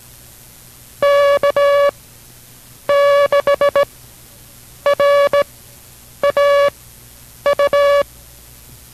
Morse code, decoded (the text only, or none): K6RAU